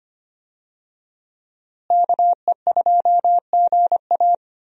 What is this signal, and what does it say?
Morse code, heard as a keyed tone, about 25 wpm, 700 Hz. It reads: KE2GA
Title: USA Call Signs - 5 Characters - Repeated 3x - 25wpm